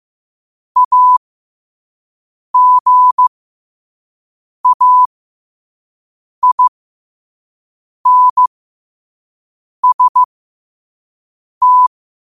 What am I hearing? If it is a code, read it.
Morse code, heard as a keyed tone, about 15 wpm, spaced as for 5 wpm, 1000 Hz. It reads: AGAINST
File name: qotdfiles / 5wpm